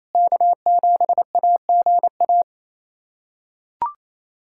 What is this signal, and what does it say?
Morse code, heard as a keyed tone, about 28 wpm, 700 Hz. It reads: K7AGA
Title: USA Callsign DE Callsign 28wpm